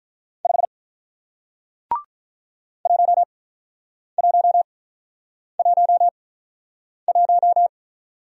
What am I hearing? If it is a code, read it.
Morse code, heard as a keyed tone, about 48 wpm, spaced as for 9 wpm, 700 Hz. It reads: 5 1111